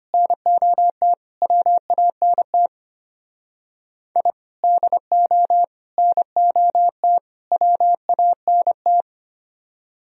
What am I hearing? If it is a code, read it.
Morse code, heard as a keyed tone, about 32 wpm, 700 Hz. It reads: NOT WANT I DO NOT WANT